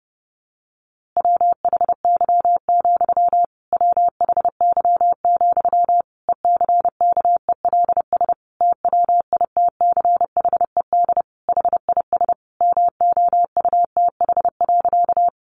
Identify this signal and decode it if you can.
Morse code, heard as a keyed tone, about 30 wpm, 700 Hz. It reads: WHY, WHY, ECKELS TWITCHED HIS MOUTH.
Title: A Sound Of Thunder 30wpm